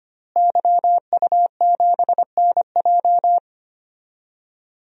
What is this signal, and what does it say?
Morse code, heard as a keyed tone, about 25 wpm, 700 Hz. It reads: YU7NJ
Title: DXCC-100 Call Signs - 5 Characters - Repeated 2x - 25wpm